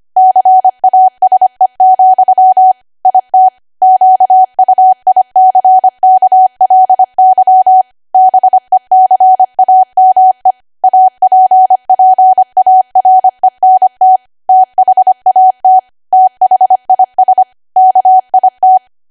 Morse code, heard as a keyed tone, about 25 wpm, 750 Hz. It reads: CASE, IT QUICKLY BECAME APPARENT THAT THIS KIT